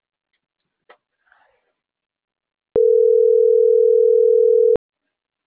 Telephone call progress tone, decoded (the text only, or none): ringback tone